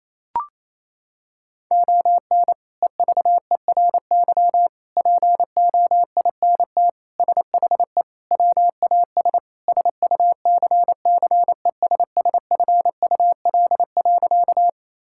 Morse code, heard as a keyed tone, about 28 wpm, 700 Hz. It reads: ON EVERY POINT SHE WAS SUCCESSFUL.
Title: Sentences from Top 1000 Words - Review Part 6 - 28wpm